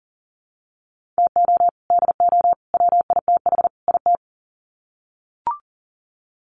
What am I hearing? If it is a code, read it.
Morse code, heard as a keyed tone, about 40 wpm, 700 Hz. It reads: TO DO WITH IT
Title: Sets of 4 Words 40wpm